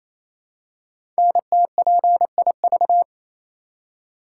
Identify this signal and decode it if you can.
Morse code, heard as a keyed tone, about 28 wpm, 700 Hz. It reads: NTPIV
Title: Code Groups - 5 Letter:Numbers - 28wpm